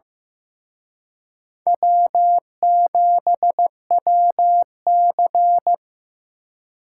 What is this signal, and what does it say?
Morse code, heard as a keyed tone, about 18 wpm, 700 Hz. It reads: W7WC
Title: Speed Racing - USA 1x2 Call Signs 15wpm